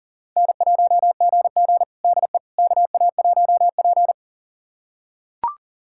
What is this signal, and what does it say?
Morse code, heard as a keyed tone, about 37 wpm, 690 Hz. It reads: N1GG DE KA1P
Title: USA Callsign DE Callsign - 4 Characters - 40wpm